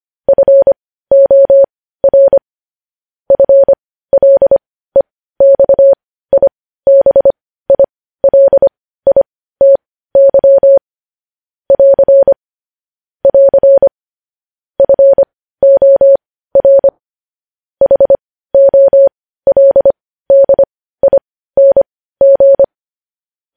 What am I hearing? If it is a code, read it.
Morse code, heard as a keyed tone, about 25 wpm, spaced as for 15 wpm, 550 Hz. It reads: FOR FLEXIBILITY <AR> <AR> FOR HOLDING
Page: news headlines in morse code at 15 wpm